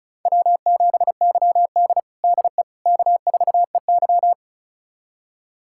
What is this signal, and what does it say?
Morse code, heard as a keyed tone, about 36 wpm, 700 Hz. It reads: W7YD DE K4EY
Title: USA Callsign DE Callsign - 4 Characters - 35wpm